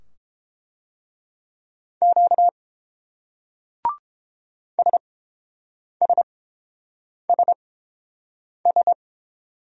Morse code, heard as a keyed tone, about 29 wpm, spaced as for 7 wpm, 700 Hz. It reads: Q SSSS